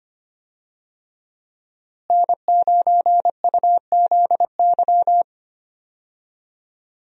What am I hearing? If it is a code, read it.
Morse code, heard as a keyed tone, about 25 wpm, 700 Hz. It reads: N9UZY
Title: USA Call Signs - 5 Characters - Repeated 3x - 25wpm